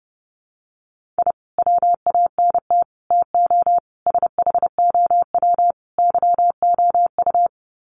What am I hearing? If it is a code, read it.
Morse code, heard as a keyed tone, about 30 wpm, 700 Hz. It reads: I WANT TO SHOW YOU
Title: Sets of 5 Words 30wpm